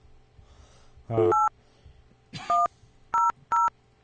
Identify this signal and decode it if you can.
DTMF, heard as telephone touch tones, tones of about 190 ms. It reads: 8100